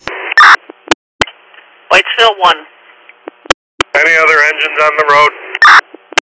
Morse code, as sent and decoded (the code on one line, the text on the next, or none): none